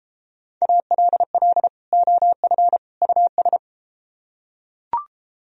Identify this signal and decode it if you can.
Morse code, heard as a keyed tone, about 33 wpm, 700 Hz. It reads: ALL OF US